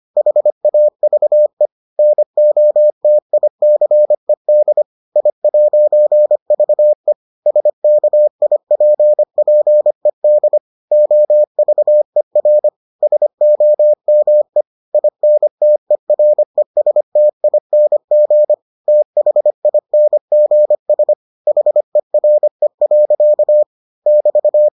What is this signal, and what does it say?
Morse code, heard as a keyed tone, about 25 wpm, 600 Hz. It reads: HAVE NOTICED I'VE SKIPPED OVER SOME INTERESTING THINGS HERE. <BT>